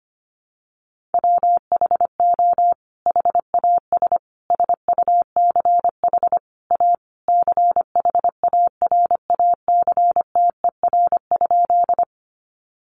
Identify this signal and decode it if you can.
Morse code, heard as a keyed tone, about 25 wpm, 700 Hz. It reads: WHO HAS SUCH A CHARACTER?
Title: Sentences from Top 1200 Words 25wpm